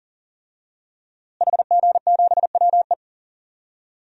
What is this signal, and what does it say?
Morse code, heard as a keyed tone, about 40 wpm, 700 Hz. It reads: HG7WE
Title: DXCC-100 Call Signs - 5 Characters - Repeated 2x - 40wpm